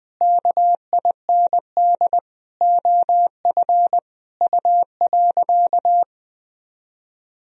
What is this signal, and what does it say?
Morse code, heard as a keyed tone, about 20 wpm, 700 Hz. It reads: KIND OF U.
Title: Sentences from Top 200 Words and Abbreviations 20wpm